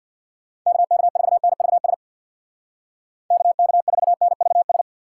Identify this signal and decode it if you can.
Morse code, heard as a keyed tone, about 64 wpm, 700 Hz. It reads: KK4NVS KK4NVS